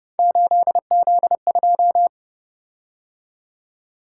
Morse code, heard as a keyed tone, about 29 wpm, 700 Hz. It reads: 8Z2
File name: Speed-Racing - Three Letter-Number Combinations - 25wpm